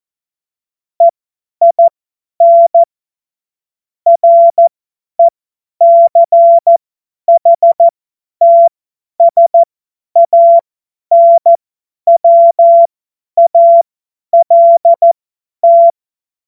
Morse code, heard as a keyed tone, about 14 wpm, spaced as for 10 wpm, 680 Hz. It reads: EIN RECHTSANWALT
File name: Morsekurs / Witze / Witz 10. Mai 2022 50 F70